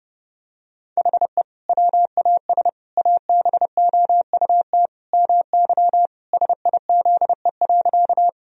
Morse code, heard as a keyed tone, about 32 wpm, 700 Hz. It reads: HE WAS ABOUT MY SIZE.
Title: Sentences from Top 1000 Words - Review Part 2 - 30wpm